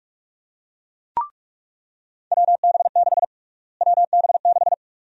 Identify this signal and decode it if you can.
Morse code, heard as a keyed tone, about 43 wpm, 700 Hz. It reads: WB6 WB6
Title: USA Call Sign Prefixes - Repeated 2x - 45wpm